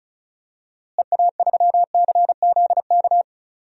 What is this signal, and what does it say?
Morse code, heard as a keyed tone, about 36 wpm, 700 Hz. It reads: EA3CZK
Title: DXCC-100 Call Signs - 6 Characters - Repeated 2x - 35wpm